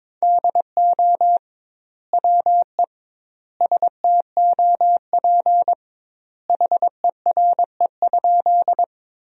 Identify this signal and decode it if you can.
Morse code, heard as a keyed tone, about 22 wpm, 700 Hz. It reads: DO WE STOP HERE?
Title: 2x Word-Spacing - Sentences from Top 300 Words - 22wpm